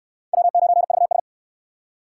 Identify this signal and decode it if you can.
Morse code, heard as a keyed tone, about 66 wpm, 700 Hz. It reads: W9VS